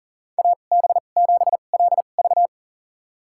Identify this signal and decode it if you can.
Morse code, heard as a keyed tone, about 40 wpm, spaced as for 29 wpm, 700 Hz. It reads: AB7LV
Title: USA Call Signs - 5 Characters - 2x Character Spacing - 40wpm